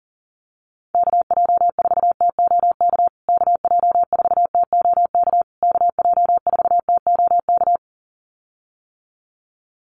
Morse code, heard as a keyed tone, about 40 wpm, 700 Hz. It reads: KJ4TOK KJ4TOK KJ4TOK
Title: USA Call Signs Repeated 3x 40wpm